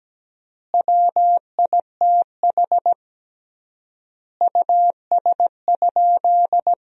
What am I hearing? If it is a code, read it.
Morse code, heard as a keyed tone, about 17 wpm, 700 Hz. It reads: WITH US?
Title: 3x Word-Spacing - Sentences from Top 200 Words - 17wpm